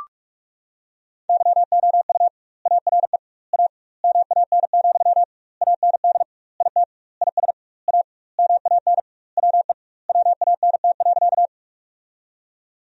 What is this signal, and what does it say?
Morse code, heard as a keyed tone, about 46 wpm, 700 Hz. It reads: YOU ARE A MAN, AND IT IS A MAN WE WANT.
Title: 2x Word-Spacing - Sentences from Top 200 Words - 45wpm